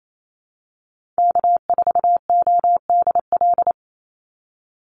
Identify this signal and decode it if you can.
Morse code, heard as a keyed tone, about 28 wpm, 700 Hz. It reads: K4ODL
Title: Call Signs Repeated 2x - 28wpm